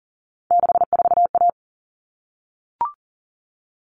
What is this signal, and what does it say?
Morse code, heard as a keyed tone, about 40 wpm, 700 Hz. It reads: 64A
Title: All Cut Numbers - 3-Digit 40wpm